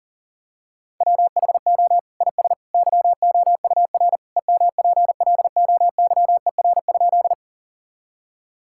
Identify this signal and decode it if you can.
Morse code, heard as a keyed tone, about 40 wpm, 700 Hz. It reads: WHO IS YOUR EMPLOYER?
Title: Sentences from Top 1600 Words 40wpm